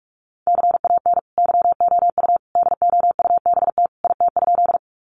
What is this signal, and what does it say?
Morse code, heard as a keyed tone, about 44 wpm, 700 Hz. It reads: CAN YOU DOUBT IT?
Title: Sentences from Top 1500 Words 45wpm